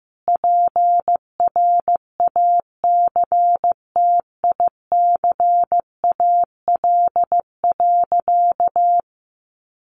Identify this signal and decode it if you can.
Morse code, heard as a keyed tone, about 15 wpm, 700 Hz. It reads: PRACTICAL.
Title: Sentences from Top 1400 Words 15wpm